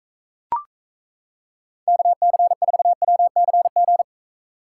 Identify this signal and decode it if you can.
Morse code, heard as a keyed tone, about 35 wpm, 700 Hz. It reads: KC4WCG